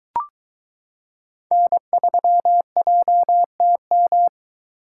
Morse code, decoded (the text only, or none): N3JTM